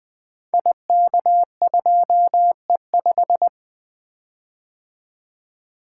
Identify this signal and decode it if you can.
Morse code, heard as a keyed tone, about 20 wpm, 700 Hz. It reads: IK2E5
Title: Code Groups - 5 Letter:Numbers - 20wpm